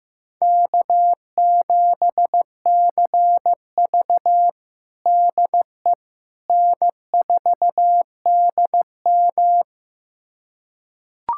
Morse code, heard as a keyed tone, about 15 wpm, 700 Hz. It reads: K7CV DE N4DM
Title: USA Callsign DE Callsign - 4 Characters - 15wpm